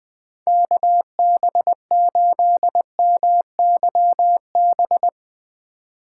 Morse code, heard as a keyed tone, about 20 wpm, 700 Hz. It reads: KB8MYB